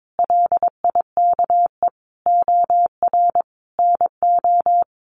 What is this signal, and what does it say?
Morse code, heard as a keyed tone, about 22 wpm, 700 Hz. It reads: LIKE OR NO
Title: Sentences from Top 1000 Words 22wpm